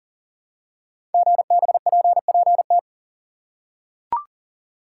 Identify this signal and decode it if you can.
Morse code, heard as a keyed tone, about 41 wpm, 700 Hz. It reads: GBPPT
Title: Code Groups - 5 Letters - 40wpm